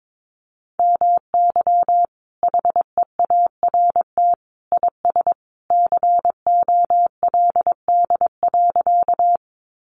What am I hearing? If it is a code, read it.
Morse code, heard as a keyed tone, about 22 wpm, 700 Hz. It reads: MY HEART IS COLD.
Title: Sentences from Top 1200 Words 22wpm